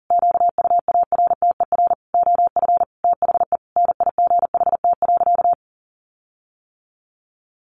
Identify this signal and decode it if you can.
Morse code, heard as a keyed tone, about 37 wpm, 700 Hz. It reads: QUARTER OF THE NIGHT.
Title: Sentences from Top 1300 Words 40wpm